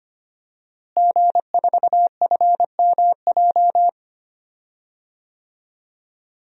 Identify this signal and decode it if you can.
Morse code, heard as a keyed tone, about 25 wpm, 700 Hz. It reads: G4FMJ
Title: DXCC-100 Call Signs - 5 Characters - Repeated 3x - 25wpm